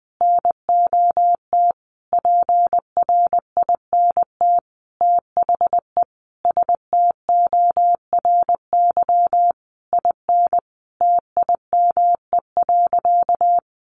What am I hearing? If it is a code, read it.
Morse code, heard as a keyed tone, about 20 wpm, 700 Hz. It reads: NOT PRINT THE STORY IN TIME.
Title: Sentences from Top 1400 Words 20wpm